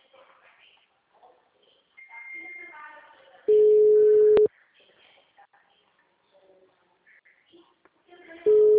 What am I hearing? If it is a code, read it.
Ringback tone (Europe, 425 Hz).